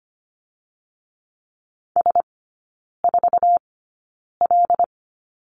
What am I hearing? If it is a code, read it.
Morse code, heard as a keyed tone, about 25 wpm, spaced as for 9 wpm, 700 Hz. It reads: S4L
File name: Lesson 15e - Number 5 - Three Characters 25wpm